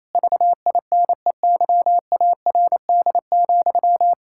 Morse code, heard as a keyed tone, about 28 wpm, 700 Hz. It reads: VINEYARD,
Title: Pangram Challenge v2 - 28wpm